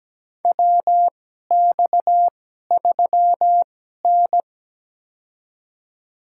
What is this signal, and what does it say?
Morse code, heard as a keyed tone, about 17 wpm, spaced as for 12 wpm, 700 Hz. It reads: WX3N
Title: USA Call Signs - 2x Character Spacing - 17wpm